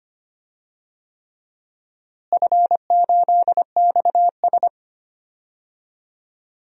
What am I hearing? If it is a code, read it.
Morse code, heard as a keyed tone, about 25 wpm, 700 Hz. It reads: F8XS
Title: DXCC-100 Call Signs - 4 Characters - Repeated 3x - 25wpm